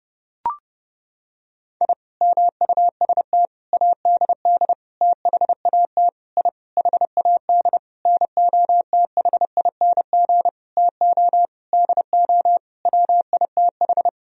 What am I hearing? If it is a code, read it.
Morse code, heard as a keyed tone, about 30 wpm, 700 Hz. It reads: I MUST ADD THAT I HAD NOTHING TO DO WITH
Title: Sentences from Top 1000 Words - Review Part 1 - 30wpm